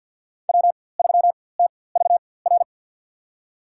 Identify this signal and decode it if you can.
Morse code, heard as a keyed tone, about 49 wpm, spaced as for 23 wpm, 700 Hz. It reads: W3TVR
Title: USA Call Signs - 5 Characters - 4x Character Spacing - 50wpm